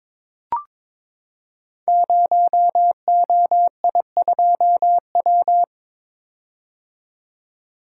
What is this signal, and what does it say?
Morse code, heard as a keyed tone, about 22 wpm, 700 Hz. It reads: 0OI2W
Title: Code Groups - 5 Letter:Numbers - 22wpm